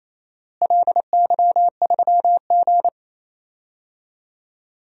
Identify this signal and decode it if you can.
Morse code, heard as a keyed tone, about 28 wpm, 700 Hz. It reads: LY3G
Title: DXCC-100 Call Signs - 4 Characters - Repeated 2x - 28wpm